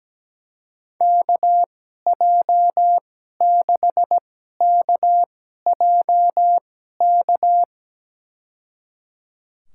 Morse code, heard as a keyed tone, about 17 wpm, spaced as for 12 wpm, 700 Hz. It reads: KJ6KJK
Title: USA Call Signs - 2x Character Spacing - 17wpm